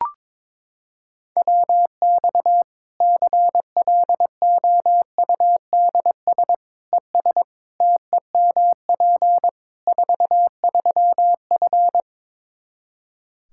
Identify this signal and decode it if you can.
Morse code, heard as a keyed tone, about 22 wpm, 700 Hz. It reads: WX CLOUDS ES TEMP 43F
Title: Practice QSO - Exchange 2 - Info - 22wpm